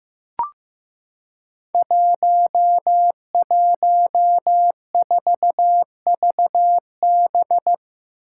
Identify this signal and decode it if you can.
Morse code, heard as a keyed tone, about 15 wpm, 700 Hz. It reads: 114VB
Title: Five Challenging Letter-Number Combinations 15wpm